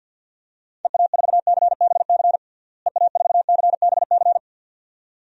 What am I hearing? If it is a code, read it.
Morse code, heard as a keyed tone, about 51 wpm, 700 Hz. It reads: EA4CBC EA4CBC
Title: DXCC-100 Call Signs - 6 Characters - Repeated 2x - 50wpm